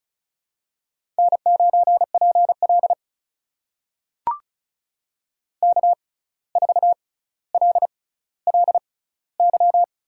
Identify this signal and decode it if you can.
Morse code, heard as a keyed tone, about 36 wpm, spaced as for 17 wpm, 700 Hz. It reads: N9PL K4LLY
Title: USA Call Signs - 6x Character Spacing - 35wpm